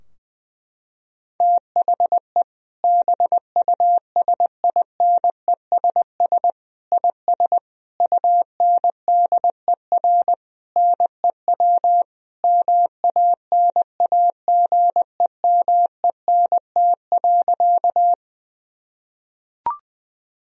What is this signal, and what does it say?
Morse code, heard as a keyed tone, about 20 wpm, 700 Hz. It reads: THE BUSINESS IS UNDER NEW MANAGEMENT.
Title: Sentences from Top 1000 Words - Review Part 4 - 20wpm